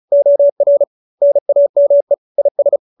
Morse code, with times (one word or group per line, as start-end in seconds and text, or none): 0.1-0.8 OR
1.2-2.1 NAME
2.4-2.8 IS